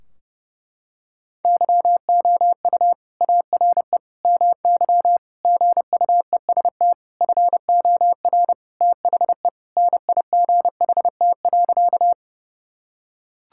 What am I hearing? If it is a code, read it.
Morse code, heard as a keyed tone, about 32 wpm, 700 Hz. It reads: YOU ARE MY GUEST FOR THE NIGHT.
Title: Sentences from Top 1600 Words 30wpm